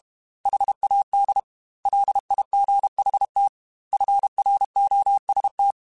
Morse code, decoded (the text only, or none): HAD LIGHT FROST